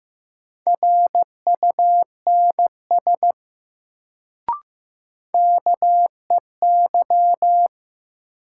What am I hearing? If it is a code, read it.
Morse code, heard as a keyed tone, about 15 wpm, 700 Hz. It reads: RUNS KEY
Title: Sentences from Top 600 Words - New Word Drill - 15wpm